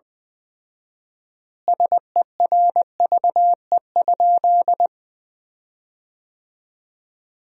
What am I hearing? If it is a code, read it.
Morse code, heard as a keyed tone, about 20 wpm, 700 Hz. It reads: SERVE?
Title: 4x Word-Spacing - Sentences from Top 400 Words - 20wpm